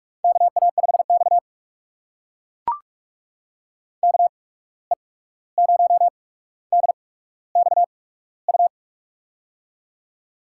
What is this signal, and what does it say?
Morse code, heard as a keyed tone, about 48 wpm, spaced as for 17 wpm, 700 Hz. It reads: KA5X KE0DXU
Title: USA Call Signs - 8x Character Spacing - 45wpm